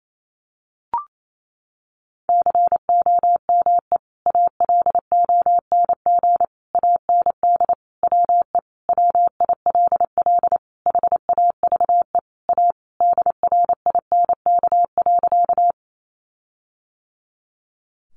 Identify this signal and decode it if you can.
Morse code, heard as a keyed tone, about 28 wpm, 700 Hz. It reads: COME ALONG AND WE WILL HAVE A DRINK.
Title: Sentences from Top 1500 Words 28wpm